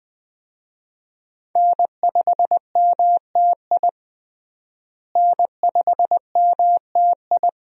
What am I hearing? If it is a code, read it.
Morse code, heard as a keyed tone, about 20 wpm, 700 Hz. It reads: N5MTI N5MTI
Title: USA Call Signs - 5 Characters - Repeated 3x - 20wpm